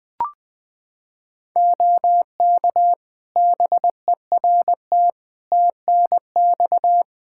Morse code, heard as a keyed tone, about 21 wpm, 700 Hz. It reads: OK BERT TNX
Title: Practice QSO - Exchange 4 - Ending - 20wpm